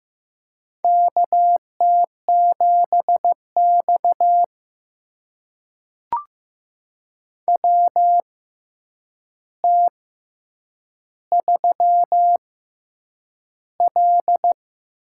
Morse code, heard as a keyed tone, about 15 wpm, spaced as for 8 wpm, 700 Hz. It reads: KT7X WT3L
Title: USA Call Signs - 4 Characters - 6x Character Spacing - 15wpm